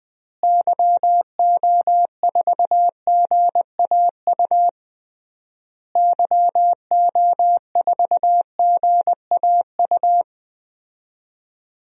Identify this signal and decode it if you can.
Morse code, heard as a keyed tone, about 20 wpm, 700 Hz. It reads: YO4GAU YO4GAU
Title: DXCC-100 Call Signs - 6 Characters - Repeated 3x - 20wpm